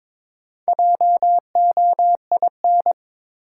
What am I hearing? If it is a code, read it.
Morse code, heard as a keyed tone, about 21 wpm, 700 Hz. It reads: JOIN
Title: Sentences from Top 1000 Words - Review Part 6 - New Word Drill - 22wpm